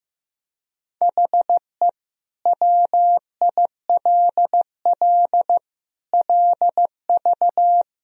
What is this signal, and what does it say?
Morse code, heard as a keyed tone, about 15 wpm, 700 Hz. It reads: HE WILL LV
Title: Sentences from Top 400 Words and Abbreviations 15wpm